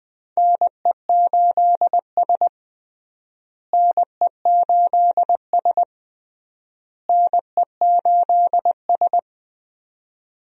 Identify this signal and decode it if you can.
Morse code, heard as a keyed tone, about 21 wpm, 700 Hz. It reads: NE8S NE8S NE8S
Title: USA Call Signs - 4 Characters - Repeated 3x - 20wpm